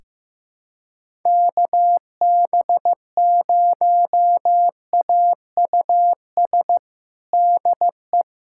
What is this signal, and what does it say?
Morse code, heard as a keyed tone, about 15 wpm, 700 Hz. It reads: KB0AUS DE
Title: USA Callsign DE Callsign 15wpm